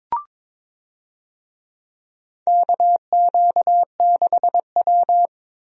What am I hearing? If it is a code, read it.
Morse code, heard as a keyed tone, about 22 wpm, 700 Hz. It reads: KQ6W
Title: USA Call Signs - 4 Characters - 22wpm